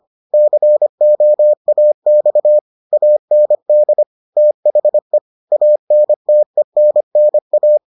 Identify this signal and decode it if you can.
Morse code, heard as a keyed tone, about 25 wpm, 600 Hz. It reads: COAX AND THE ANTENNA